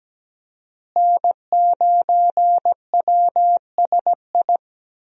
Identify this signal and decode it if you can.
Morse code, heard as a keyed tone, about 17 wpm, 700 Hz. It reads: N9WSI
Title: USA Call Signs - 5 Characters - Repeated 3x - 17wpm